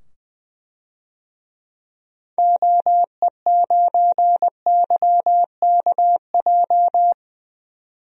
Morse code, heard as a keyed tone, about 20 wpm, 700 Hz. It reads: OE9YKJ
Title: DXCC-100 Call Signs - 6 Characters - 20wpm